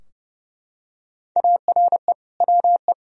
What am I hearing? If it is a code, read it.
Morse code, heard as a keyed tone, about 30 wpm, 700 Hz. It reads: ARE WE
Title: Sentences from Top 1000 Words - Review Part 3 - 30wpm